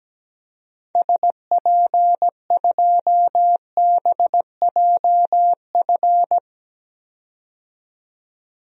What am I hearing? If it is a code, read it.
Morse code, heard as a keyed tone, about 17 wpm, 700 Hz. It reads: SP2BJF
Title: DXCC-100 Call Signs - 6 Characters - 17wpm